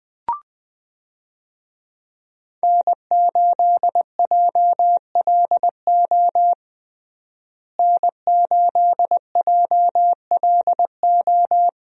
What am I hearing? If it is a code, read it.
Morse code, heard as a keyed tone, about 20 wpm, 700 Hz. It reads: N8JLO N8JLO
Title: USA Call Signs - 5 Characters - Repeated 3x - 20wpm